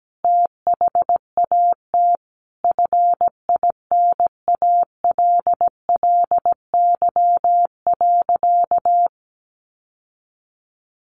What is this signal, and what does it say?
Morse code, heard as a keyed tone, about 17 wpm, 700 Hz. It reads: THAT FINALLY.
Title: Sentences from Top 1000 Words 17wpm